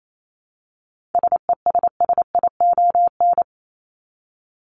Morse code, heard as a keyed tone, about 28 wpm, 700 Hz. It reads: SESSION
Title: Top 1501-1750 Words 28wpm